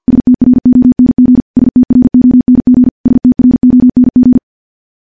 Morse code, heard as a keyed tone, about 50 wpm, 250 Hz. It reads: STRONG STRONG STRONG
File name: morse / morse-training-50WPM